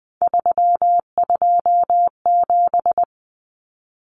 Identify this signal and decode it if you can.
Morse code, heard as a keyed tone, about 20 wpm, 700 Hz. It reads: 327
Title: Common Cut Numbers - 3-Digit 20wpm